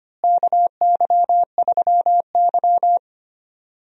Morse code, heard as a keyed tone, about 25 wpm, 700 Hz. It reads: KY3Y
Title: USA Call Signs - 4 Characters - Repeated 2x - 25wpm